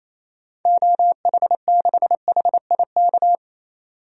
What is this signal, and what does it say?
Morse code, heard as a keyed tone, about 28 wpm, 700 Hz. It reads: OH6HIK
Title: DXCC-100 Call Signs - 6 Characters - 28wpm